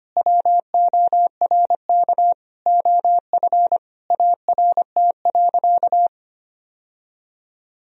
Morse code, heard as a keyed tone, about 25 wpm, 700 Hz. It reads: WORK OF ART.
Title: Sentences from Top 1000 Words - Review Part 3 - 25wpm